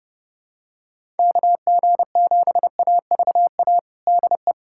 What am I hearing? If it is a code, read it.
Morse code, heard as a keyed tone, about 30 wpm, 700 Hz. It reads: KG7AVA DE
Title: USA Callsign DE Callsign - 6 Characters - 30wpm